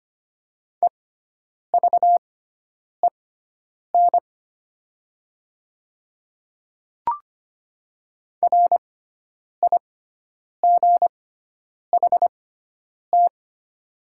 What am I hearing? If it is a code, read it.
Morse code, heard as a keyed tone, about 25 wpm, spaced as for 9 wpm, 700 Hz. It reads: EVEN RIGHT
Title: 6x Character Spacing - Top 300 Words - 25wpm